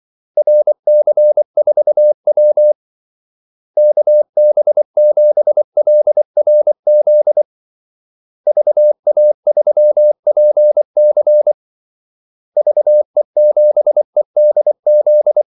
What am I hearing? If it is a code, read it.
Morse code, heard as a keyed tone, about 24 wpm, 600 Hz. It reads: RC4W KB7LRZ VA3PC VE7EDZ
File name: Callsigns-24wpm